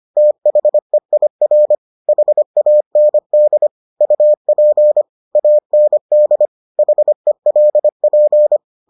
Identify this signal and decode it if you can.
Morse code, heard as a keyed tone, about 25 wpm, 600 Hz. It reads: THEIR HAND UP AND HELP